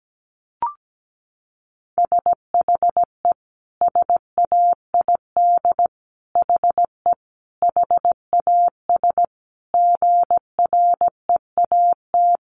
Morse code, heard as a keyed tone, about 17 wpm, 700 Hz. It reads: SHE SAID HE HAS GREAT
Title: Sentences from Top 1300 Words 17wpm